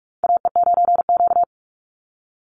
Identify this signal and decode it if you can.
Morse code, heard as a keyed tone, about 44 wpm, 700 Hz. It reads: AE9Q
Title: NA SOTA Chaser Callsigns 45wpm